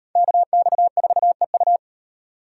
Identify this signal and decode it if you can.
Morse code, heard as a keyed tone, about 38 wpm, 700 Hz. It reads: KX4EU